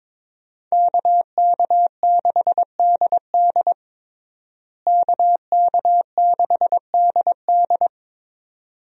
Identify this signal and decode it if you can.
Morse code, heard as a keyed tone, about 22 wpm, 700 Hz. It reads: KK6DD KK6DD
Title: USA Call Signs - 5 Characters - Repeated 2x - 22wpm